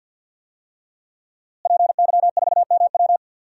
Morse code, heard as a keyed tone, about 46 wpm, 700 Hz. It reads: PY4MW
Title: DXCC-100 Call Signs - 5 Characters - Repeated 3x - 50wpm